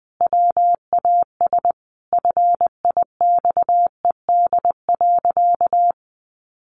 Morse code, heard as a keyed tone, about 20 wpm, 700 Hz. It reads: WAS FIXED.
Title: Sentences from Top 1100 Words 20wpm